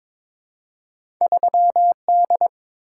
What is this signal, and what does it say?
Morse code, heard as a keyed tone, about 22 wpm, 700 Hz. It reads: 3D
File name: practice / 22wpm / Two Letter-Number Combinations - 22wpm